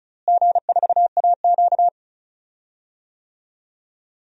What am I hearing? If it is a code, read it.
Morse code, heard as a keyed tone, about 34 wpm, 700 Hz. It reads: G4AQ
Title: DXCC-100 Call Signs - 4 Characters - Repeated 2x - 35wpm